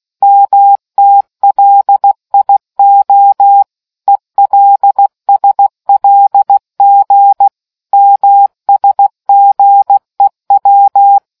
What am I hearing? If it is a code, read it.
Morse code, heard as a keyed tone, about 16 wpm, 800 Hz. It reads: MTLIO ELSLG MSGEW